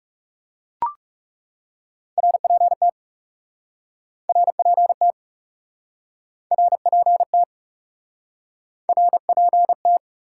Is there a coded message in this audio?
Yes: Morse code from 2.2 to 10.0 s, heard as a keyed tone, about 40 wpm, 700 Hz. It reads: RPT RPT RPT RPT